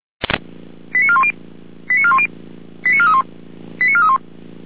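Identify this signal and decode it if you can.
Five-tone selective call (ZVEI), heard as quick runs of five tones, tones of about 70 ms. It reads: 89410 89410 89431 89431
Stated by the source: ZVEI five-tone call received on SDR, from audio server of feuerwehr-orlamuende (fire brigade Orlamünde).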